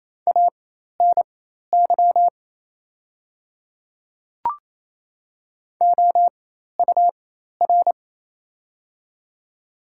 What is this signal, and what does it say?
Morse code, heard as a keyed tone, about 27 wpm, spaced as for 13 wpm, 690 Hz. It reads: ANY OUR
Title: 4x Character Spacing - Top 200 Words - 28wpm